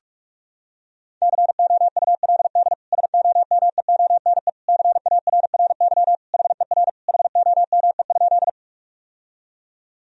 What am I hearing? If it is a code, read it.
Morse code, heard as a keyed tone, about 46 wpm, 700 Hz. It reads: COULD SOMEONE CARRY HER HOME?
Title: Sentences from Top 1000 Words - Review Part 4 - 45wpm